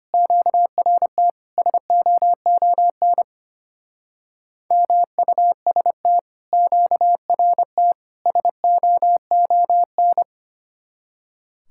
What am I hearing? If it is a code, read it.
Morse code, heard as a keyed tone, about 32 wpm, 700 Hz. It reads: QRT SOON MUST QRT SOON